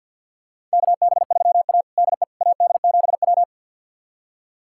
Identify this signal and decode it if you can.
Morse code, heard as a keyed tone, about 53 wpm, 690 Hz. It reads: KB3A DE AD7W
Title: USA Callsign DE Callsign - 4 Characters - 50wpm